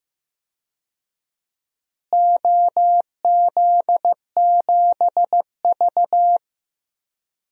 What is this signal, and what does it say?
Morse code, heard as a keyed tone, about 15 wpm, 700 Hz. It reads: OZ7V
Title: DXCC-100 Call Signs - 4 Characters - Repeated 3x - 15wpm